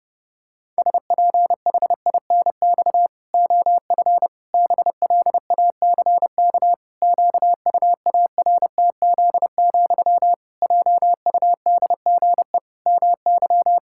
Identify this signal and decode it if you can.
Morse code, heard as a keyed tone, about 32 wpm, 700 Hz. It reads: SPHINX OF BLACK QUARTZ, JUDGE MY
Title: Pangram Challenge v2 - 30wpm